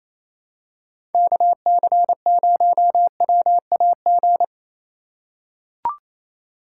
Morse code, heard as a keyed tone, about 28 wpm, 700 Hz. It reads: KC0WAG
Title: USA Call Signs - 3x Character Spacing - 28wpm